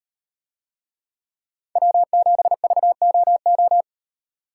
Morse code, heard as a keyed tone, about 38 wpm, 700 Hz. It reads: W7VOO